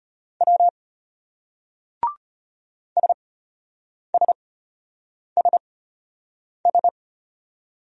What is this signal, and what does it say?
Morse code, heard as a keyed tone, about 34 wpm, spaced as for 8 wpm, 700 Hz. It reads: W SSSS